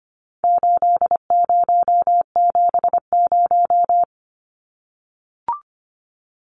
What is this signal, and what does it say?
Morse code, heard as a keyed tone, about 25 wpm, 700 Hz. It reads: 8070
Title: Numbers - 4-Digit 25wpm